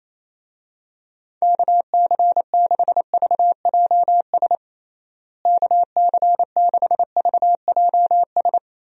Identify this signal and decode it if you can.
Morse code, heard as a keyed tone, about 27 wpm, 700 Hz. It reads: KC6VJS KC6VJS
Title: USA Call Signs - 6 Characters - Repeated 2x - 28wpm